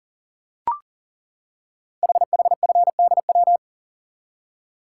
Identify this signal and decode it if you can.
Morse code, heard as a keyed tone, about 42 wpm, 700 Hz. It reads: HHFDW
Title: Code Groups - 5 Letters - 40wpm